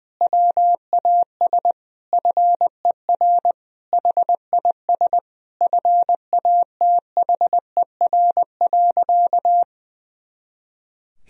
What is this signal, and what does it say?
Morse code, heard as a keyed tone, about 21 wpm, 700 Hz. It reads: WAS FER HIS FATHER.
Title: Sentences from Top 500 Words and Abbreviations 20wpm